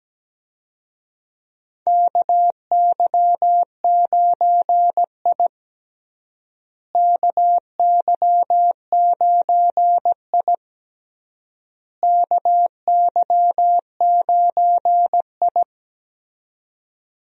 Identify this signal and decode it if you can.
Morse code, heard as a keyed tone, about 17 wpm, 700 Hz. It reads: KY9I KY9I KY9I
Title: USA Call Signs - 4 Characters - Repeated 3x - 17wpm